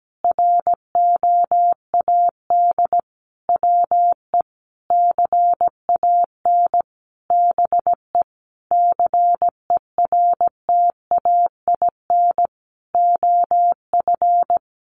Morse code, heard as a keyed tone, about 17 wpm, 700 Hz. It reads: ROAD WE CAN BE CERTAIN OF
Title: Sentences from Top 1100 Words 17wpm